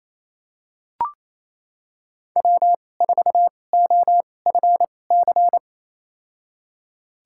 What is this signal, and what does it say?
Morse code, heard as a keyed tone, about 29 wpm, spaced as for 21 wpm, 700 Hz. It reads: W4OFC
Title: USA Call Signs - 5 Characters - 2x Character Spacing - 28wpm